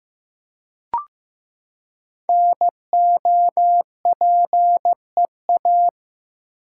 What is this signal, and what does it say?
Morse code, heard as a keyed tone, about 15 wpm, 700 Hz. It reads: NOPEA
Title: Code Groups - 5 Letters - 15wpm